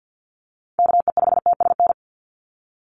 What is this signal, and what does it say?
Morse code, heard as a keyed tone, about 50 wpm, 700 Hz. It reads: KE5TSN
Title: USA Call Signs 50wpm